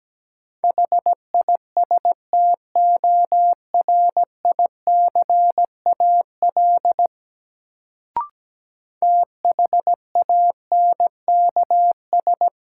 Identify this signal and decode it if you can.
Morse code, heard as a keyed tone, about 17 wpm, 700 Hz. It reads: HISTORICAL THANKS
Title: Sentences from Top 600 Words - New Word Drill - 17wpm